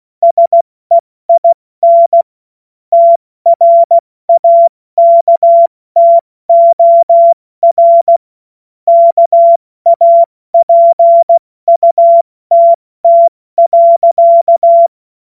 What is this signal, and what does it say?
Morse code, heard as a keyed tone, about 16 wpm, 680 Hz. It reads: SEIN TRAKTOR KAPUTT.
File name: Witz-08-Juli-2025-70-F80